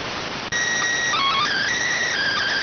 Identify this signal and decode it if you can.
SSTV (slow-scan television), PD 290.